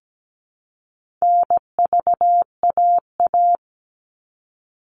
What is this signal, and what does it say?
Morse code, heard as a keyed tone, about 17 wpm, 700 Hz. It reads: NVAA
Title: All Cut Numbers - 4-Digit 17wpm